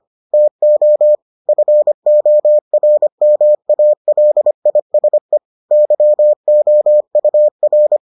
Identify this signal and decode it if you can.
Morse code, heard as a keyed tone, about 25 wpm, 600 Hz. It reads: TO FORMALISE YOUR